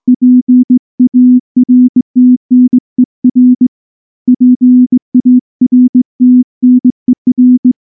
Morse code, heard as a keyed tone, about 20 wpm, 250 Hz. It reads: PARTNER PARTNER
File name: morse-training-20WPM